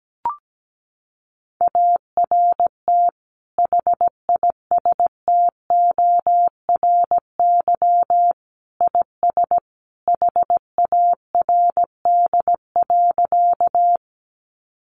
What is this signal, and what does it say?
Morse code, heard as a keyed tone, about 17 wpm, 700 Hz. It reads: ART HISTORY IS HARD.
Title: Sentences from Top 600 Words 17wpm